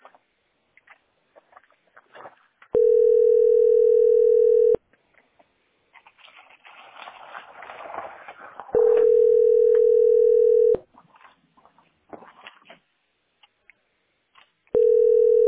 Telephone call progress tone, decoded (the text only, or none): ringback tone